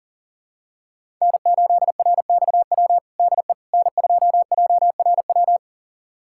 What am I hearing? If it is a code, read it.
Morse code, heard as a keyed tone, about 40 wpm, 700 Hz. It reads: N8RXW DE N2JRW
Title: USA Callsign DE Callsign - 5 Characters - 40wpm